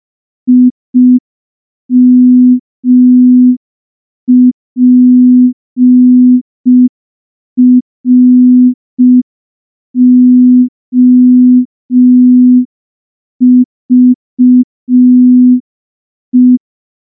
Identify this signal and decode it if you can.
Morse code, heard as a keyed tone, about 5 wpm, 250 Hz. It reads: IMPROVE
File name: morse / morse-training-05WPM